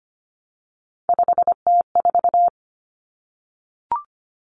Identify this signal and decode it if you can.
Morse code, heard as a keyed tone, about 25 wpm, 700 Hz. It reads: 5T4
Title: Common Cut Numbers - 3-Digit 25wpm